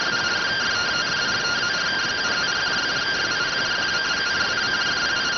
Two-tone teleprinter signal, about 170 Hz shift.